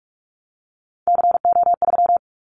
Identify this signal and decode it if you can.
Morse code, heard as a keyed tone, about 46 wpm, 700 Hz. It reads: CO3
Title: Callsign Prefixes for Top 100 DXCC - 45wpm